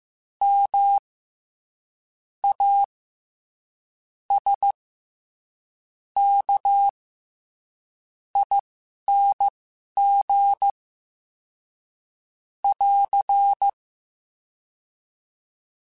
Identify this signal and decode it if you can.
Morse code, heard as a keyed tone, about 15 wpm, spaced as for 11 wpm, 800 Hz. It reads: M A S K ING <AR>
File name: Special INGs-mp3s / Special INGs-mp3s / SlowToFaster ing @10